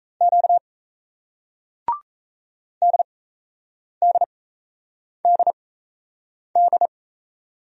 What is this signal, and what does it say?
Morse code, heard as a keyed tone, about 38 wpm, spaced as for 8 wpm, 700 Hz. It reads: Q DDDD